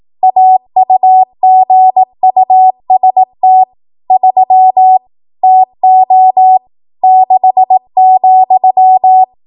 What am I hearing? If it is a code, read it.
Morse code, heard as a keyed tone, about 18 wpm, 750 Hz. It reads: AUGUST 3 TO 6,
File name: Archive / 18 WPM / 18WPM